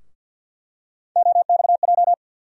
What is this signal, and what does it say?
Morse code, heard as a keyed tone, about 52 wpm, 700 Hz. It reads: OXJ
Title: USA Call Sign Postfixes - 3 Letters 50wpm